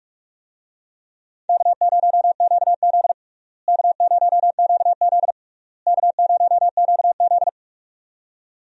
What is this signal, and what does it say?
Morse code, heard as a keyed tone, about 46 wpm, 700 Hz. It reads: K0QZ K0QZ K0QZ
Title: USA Call Signs - 4 Characters - Repeated 3x - 45wpm